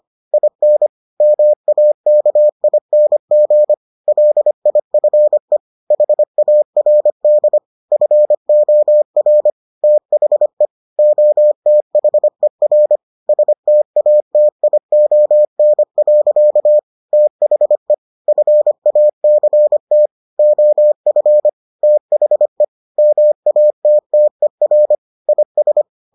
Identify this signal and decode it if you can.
Morse code, heard as a keyed tone, about 25 wpm, 600 Hz. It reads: IN MAKING LIFE HARD FOR THE OTHER STATION. THE FACT OF THE MATTER IS